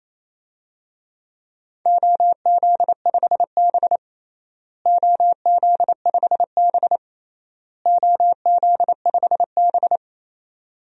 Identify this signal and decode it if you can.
Morse code, heard as a keyed tone, about 28 wpm, 700 Hz. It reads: OZ5B OZ5B OZ5B
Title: DXCC-100 Call Signs - 4 Characters - Repeated 3x - 28wpm